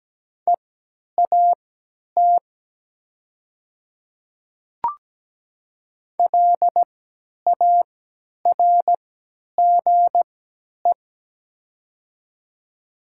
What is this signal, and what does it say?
Morse code, heard as a keyed tone, about 17 wpm, spaced as for 10 wpm, 700 Hz. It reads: EAT LARGE
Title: 3x Character Spacing - Top 300 Words - 17wpm